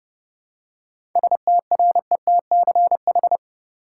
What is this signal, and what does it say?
Morse code, heard as a keyed tone, about 30 wpm, 700 Hz. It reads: STRETCH
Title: Seven Letter Words - 30wpm